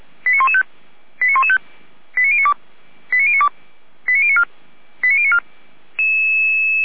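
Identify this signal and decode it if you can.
Five-tone selective call (ZVEI), heard as quick runs of five tones, tones of about 70 ms. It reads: 89115 89115 89092 89092 89094 89094
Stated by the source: ZVEI five-tone call received with an SDR.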